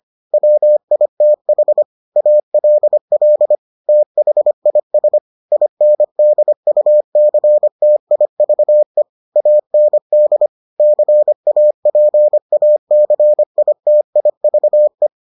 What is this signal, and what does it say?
Morse code, heard as a keyed tone, about 25 wpm, 600 Hz. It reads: WITH ALL THIS INDUCTIVE AND CAPACITIVE